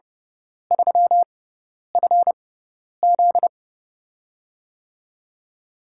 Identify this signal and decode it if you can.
Morse code, heard as a keyed tone, about 30 wpm, spaced as for 10 wpm, 700 Hz. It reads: 3FZ
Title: USA Call Signs - 4 Characters - 6x Character Spacing - 30wpm